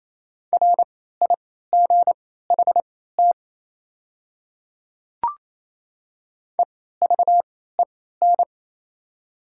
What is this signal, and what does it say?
Morse code, heard as a keyed tone, about 27 wpm, spaced as for 16 wpm, 690 Hz. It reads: RIGHT EVEN
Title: 3x Character Spacing - Top 200 Words - 28wpm